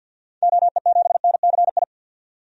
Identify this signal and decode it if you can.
Morse code, heard as a keyed tone, about 50 wpm, 700 Hz. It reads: OE7NCI
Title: DXCC-100 Call Signs - 6 Characters - 50wpm